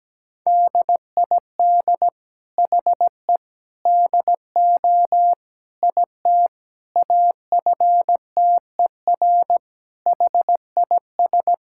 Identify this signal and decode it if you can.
Morse code, heard as a keyed tone, about 17 wpm, 700 Hz. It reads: DID HE DO IT AFTER HIS
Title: Sentences from Top 1600 Words 17wpm